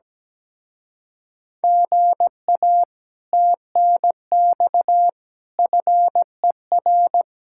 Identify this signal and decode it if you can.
Morse code, heard as a keyed tone, about 17 wpm, 700 Hz. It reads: GA TNX FER
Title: Speed-Racing Common QSO Phrases - 17wpm